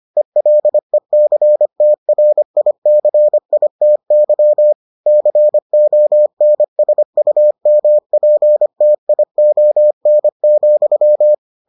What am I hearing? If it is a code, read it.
Morse code, heard as a keyed tone, about 25 wpm, 600 Hz. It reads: ELECTRICITY CONSUMPTION,